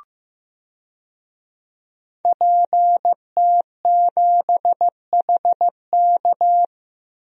Morse code, heard as a keyed tone, about 15 wpm, 700 Hz. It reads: PT7HK
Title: DXCC-100 Call Signs - 5 Characters - 15wpm